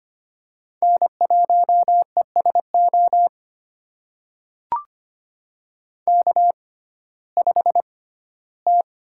Morse code, heard as a keyed tone, about 25 wpm, spaced as for 15 wpm, 700 Hz. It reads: N1ESO K5T
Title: USA Call Signs - 5 Characters - 6x Character Spacing - 25wpm